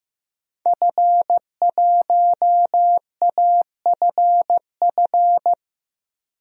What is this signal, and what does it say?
Morse code, heard as a keyed tone, about 15 wpm, 700 Hz. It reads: F1AFF
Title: DXCC-100 Call Signs - 5 Characters - 15wpm